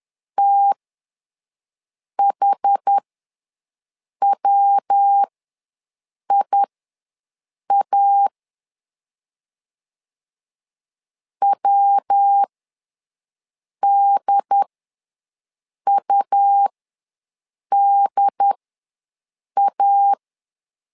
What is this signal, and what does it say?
Morse code, heard as a keyed tone, about 10 wpm, spaced as for 5 wpm, 800 Hz. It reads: THWIA WDUDA
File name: LEZIONI / 3 MWRO / 3 MW ws 10wpm 2213